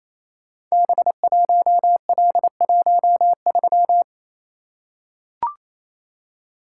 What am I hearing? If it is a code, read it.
Morse code, heard as a keyed tone, about 28 wpm, 700 Hz. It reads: B1L13
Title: Code Groups - 5 Letter:Numbers - 28wpm